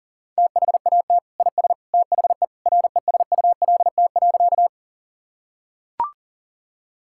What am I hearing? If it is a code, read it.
Morse code, heard as a keyed tone, about 40 wpm, 700 Hz. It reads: THAT IS THE RESULT.
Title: Sentences from Top 1000 Words - Review Part 5 - 40wpm